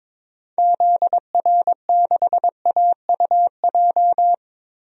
Morse code, heard as a keyed tone, about 22 wpm, 700 Hz. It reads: ZR6AUJ
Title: DXCC-100 Call Signs - 6 Characters - 22wpm